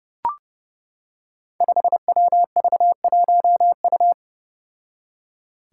Morse code, heard as a keyed tone, about 30 wpm, 700 Hz. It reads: 5WV1U
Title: Five Challenging Letter-Number Combinations 30wpm